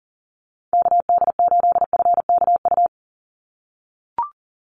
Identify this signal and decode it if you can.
Morse code, heard as a keyed tone, about 39 wpm, 700 Hz. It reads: KD8FKU
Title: USA Call Signs 40wpm